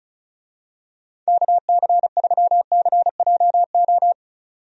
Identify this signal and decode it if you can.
Morse code, heard as a keyed tone, about 34 wpm, 700 Hz. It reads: KC3CJO
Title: USA Call Signs - 6 Characters - 35wpm